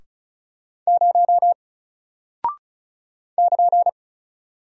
Morse code, heard as a keyed tone, about 35 wpm, 700 Hz. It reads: 0 <KN>
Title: Single Character (Letter:Number:Punctuation:Prosign) 35wpm